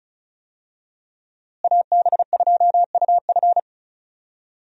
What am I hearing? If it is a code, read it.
Morse code, heard as a keyed tone, about 34 wpm, 700 Hz. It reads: AB2UF
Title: USA Call Signs - 5 Characters - Repeated 2x - 35wpm